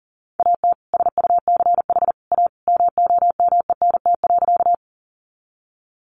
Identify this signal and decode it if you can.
Morse code, heard as a keyed tone, about 40 wpm, 700 Hz. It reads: AT SUCH A MOMENT.
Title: Sentences from Top 1000 Words 40wpm